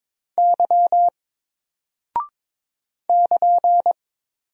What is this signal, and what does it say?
Morse code, heard as a keyed tone, about 22 wpm, 700 Hz. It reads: Y <KN>
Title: Single Character (Letter:Number:Punctuation:Prosign) 22wpm